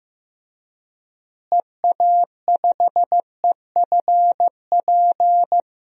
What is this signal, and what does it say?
Morse code, heard as a keyed tone, about 15 wpm, 700 Hz. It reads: EA5EFP
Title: DXCC-100 Call Signs - 6 Characters - Repeated 2x - 15wpm